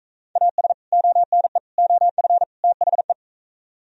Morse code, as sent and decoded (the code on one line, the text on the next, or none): .- ... / --- -. / --- ..-. / - .... .
AS ONE OF THE